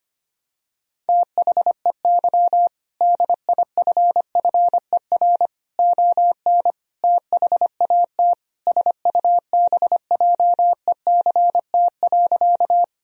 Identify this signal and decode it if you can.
Morse code, heard as a keyed tone, about 25 wpm, 700 Hz. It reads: THEY DIFFER ON THAT SUBJECT.
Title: Sentences from Top 1000 Words - Review Part 1 - 25wpm